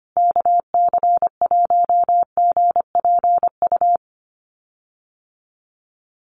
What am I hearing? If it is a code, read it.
Morse code, heard as a keyed tone, about 25 wpm, 700 Hz. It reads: KC1GPU
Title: USA Call Signs Repeated 2x 25wpm